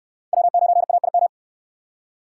Morse code, heard as a keyed tone, about 57 wpm, 700 Hz. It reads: W9AIM